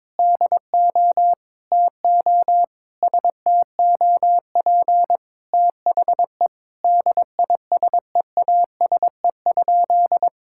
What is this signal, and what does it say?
Morse code, heard as a keyed tone, about 22 wpm, 700 Hz. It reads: DO TO STOP THE DISEASE?
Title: Sentences from Top 1000 Words - Review Part 6 - 22wpm